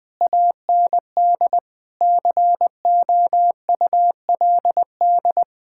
Morse code, heard as a keyed tone, about 20 wpm, 700 Hz. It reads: AND COULD
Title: Sentences with 7-Word Phrases - Spoken Beginning - 20wpm